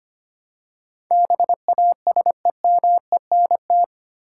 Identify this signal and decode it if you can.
Morse code, heard as a keyed tone, about 25 wpm, 700 Hz. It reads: BASEMENT
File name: 8 Letter Words - First 3 Letters Given - 25wpm